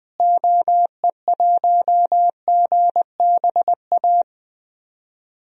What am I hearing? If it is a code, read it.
Morse code, heard as a keyed tone, about 20 wpm, 700 Hz. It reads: OE1GBA
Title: DXCC-100 Call Signs - 6 Characters - Repeated 3x - 20wpm